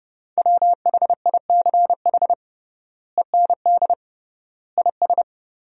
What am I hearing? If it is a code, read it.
Morse code, heard as a keyed tone, about 31 wpm, 700 Hz. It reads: WHICH END IS
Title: 3x Word-Spacing - Sentences from Top 300 Words - 30wpm